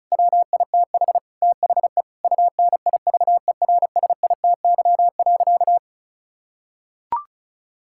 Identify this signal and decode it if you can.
Morse code, heard as a keyed tone, about 36 wpm, 700 Hz. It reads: WITH THE UNIVERSITY.
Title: Sentences from Top 1000 Words - Review Part 6 - 35wpm